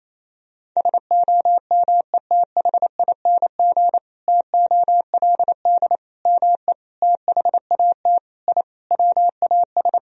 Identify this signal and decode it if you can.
Morse code, heard as a keyed tone, about 28 wpm, 700 Hz. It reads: SOMETHING TOLD ME THAT I WAS
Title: Sentences from Top 1000 Words - Review Part 1 - 28wpm